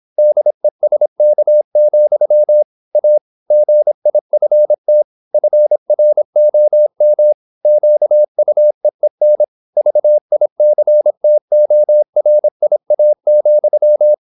Morse code, heard as a keyed tone, about 26 wpm, 600 Hz. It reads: DESK, A GIFT FROM QUEEN VICTORIA,
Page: Never 26wpm | CW med Gnister